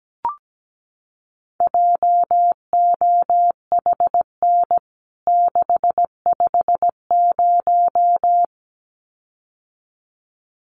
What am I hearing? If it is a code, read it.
Morse code, heard as a keyed tone, about 17 wpm, 700 Hz. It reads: JOHN 650
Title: CWT Exchange 17wpm